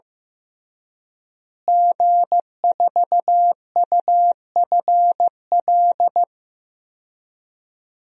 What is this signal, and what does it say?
Morse code, heard as a keyed tone, about 15 wpm, 700 Hz. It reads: G4UFL
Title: DXCC-100 Call Signs - 5 Characters - Repeated 3x - 15wpm